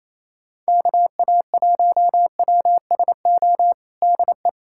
Morse code, heard as a keyed tone, about 28 wpm, 700 Hz. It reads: KA1WSO DE
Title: USA Callsign DE Callsign - 6 Characters - 28wpm